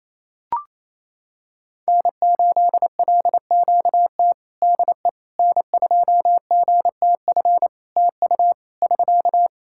Morse code, heard as a keyed tone, about 28 wpm, 700 Hz. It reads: N8LQT DE N2GTF TU <SK>
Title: K1USN SST QSO Practice 28wpm